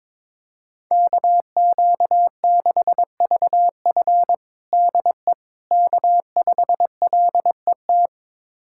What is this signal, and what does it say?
Morse code, heard as a keyed tone, about 22 wpm, 700 Hz. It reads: KQ6VF DE K5LET
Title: USA Callsign DE Callsign - 5 Characters - 22wpm